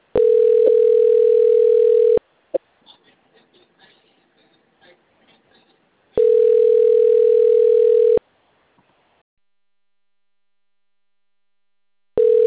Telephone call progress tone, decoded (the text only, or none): ringback tone